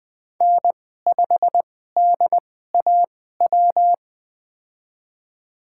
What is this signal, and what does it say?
Morse code, heard as a keyed tone, about 20 wpm, spaced as for 15 wpm, 700 Hz. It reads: N5DAW
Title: USA Call Signs - 5 Characters - 2x Character Spacing - 20wpm